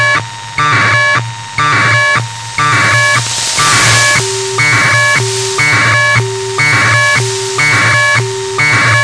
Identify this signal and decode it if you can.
Packet radio (AX.25, as in APRS).
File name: Sohla-1 23h48 utc 1200bd packet NormalModeby ja0caw